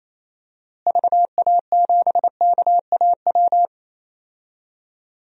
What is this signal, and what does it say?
Morse code, heard as a keyed tone, about 28 wpm, 700 Hz. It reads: VA7KAW
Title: DXCC-100 Call Signs - 6 Characters - Repeated 2x - 28wpm